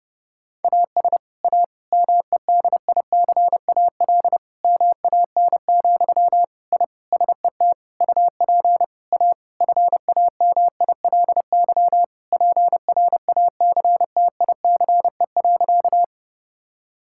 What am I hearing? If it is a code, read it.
Morse code, heard as a keyed tone, about 30 wpm, 700 Hz. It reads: AS A MEDICAL MAN, I SET UP A FAMILY PRACTICE.
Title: Sentences from Top 1000 Words - Review Part 5 - 30wpm